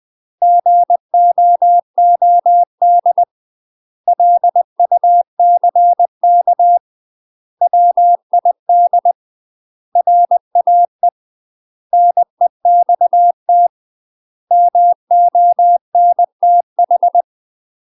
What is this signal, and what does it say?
Morse code, heard as a keyed tone, about 20 wpm, 700 Hz. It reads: GOOD LUCK WID RAE NEXT MONTH
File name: UK MASTER QSOs TRAINING/20WPM